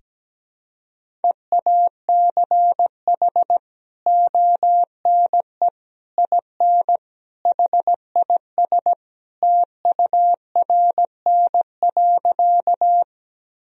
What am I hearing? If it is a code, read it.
Morse code, heard as a keyed tone, about 17 wpm, 700 Hz. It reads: EACH ONE IN HIS TURN.
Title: Sentences from Top 1000 Words - Review Part 1 - 17wpm